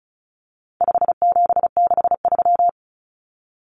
Morse code, heard as a keyed tone, about 34 wpm, 700 Hz. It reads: 5763
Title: Common Cut Numbers - 4-Digit 35wpm